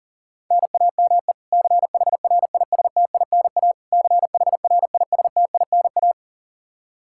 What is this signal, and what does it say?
Morse code, heard as a keyed tone, about 40 wpm, 700 Hz. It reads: NAME CHRISTINA CHRISTINA
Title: K1USN SST QSO Practice 40wpm